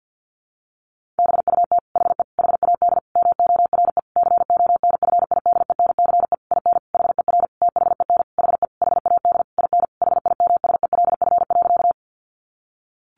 Morse code, heard as a keyed tone, about 50 wpm, 700 Hz. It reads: BUT HE HAD MORE CONFIDENCE IN HER THEN HE HAD IN HIMSELF.
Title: Sentences from Top 1500 Words 50wpm